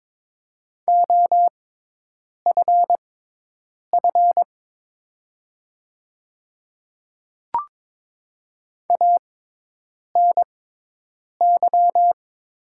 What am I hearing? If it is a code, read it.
Morse code, heard as a keyed tone, about 23 wpm, spaced as for 8 wpm, 700 Hz. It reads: OFF ANY
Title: 6x Character Spacing - Top 200 Words - 22wpm